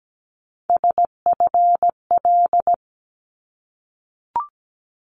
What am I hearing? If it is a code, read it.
Morse code, heard as a keyed tone, about 17 wpm, 700 Hz. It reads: SFL
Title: ARRL Sections 17wpm